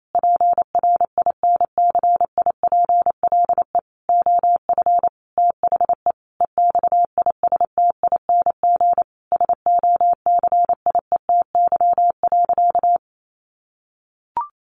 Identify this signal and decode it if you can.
Morse code, heard as a keyed tone, about 28 wpm, 700 Hz. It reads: PRINCIPLE OF THE EXISTING SOCIETY.
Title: Sentences from Top 1500 Words 28wpm